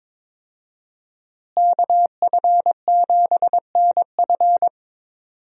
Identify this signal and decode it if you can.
Morse code, heard as a keyed tone, about 22 wpm, 700 Hz. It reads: KF7NF
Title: USA Call Signs - 5 Characters - Repeated 3x - 22wpm